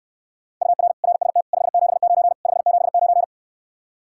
Morse code, heard as a keyed tone, about 69 wpm, 700 Hz. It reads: UR RST 579 579